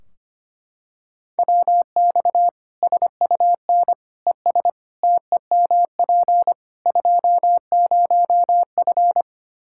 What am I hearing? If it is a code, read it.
Morse code, heard as a keyed tone, about 25 wpm, 700 Hz. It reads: WX SUN ES TEMP 20F
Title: Practice QSO - Exchange 2 - Info - 25wpm